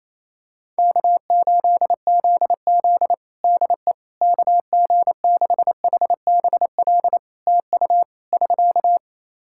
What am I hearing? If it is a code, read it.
Morse code, heard as a keyed tone, about 28 wpm, 700 Hz. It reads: K8ZZ DE KG6HBL TU <SK>
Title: Practice QSO - Exchange 4 - Ending - 28wpm